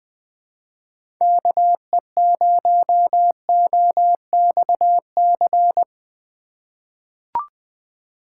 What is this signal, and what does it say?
Morse code, heard as a keyed tone, about 20 wpm, 700 Hz. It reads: KE0OXC
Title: USA Call Signs - 6 Characters - 4x Character Spacing - 20wpm